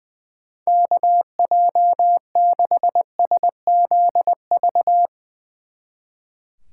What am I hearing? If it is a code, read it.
Morse code, heard as a keyed tone, about 22 wpm, 700 Hz. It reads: KJ6SZV
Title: Speed Racing - USA 2x3 Call Signs 20wpm